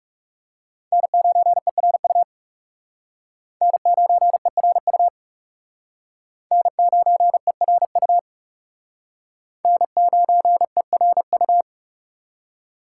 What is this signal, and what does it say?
Morse code, heard as a keyed tone, about 39 wpm, 700 Hz. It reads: N9ERU N9ERU N9ERU N9ERU